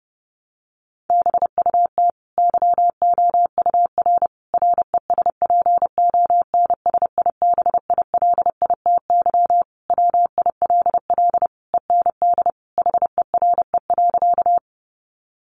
Morse code, heard as a keyed tone, about 30 wpm, 700 Hz. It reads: BUT YOUR RESPONSIBILITY WILL END HERE.
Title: Sentences from Top 1200 Words 30wpm